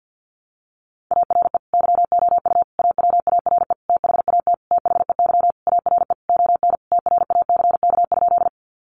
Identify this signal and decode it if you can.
Morse code, heard as a keyed tone, about 48 wpm, 700 Hz. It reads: ARE YOU AWARE THAT THEY ARE ON TRACK?
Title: Sentences from Top 1200 Words 50wpm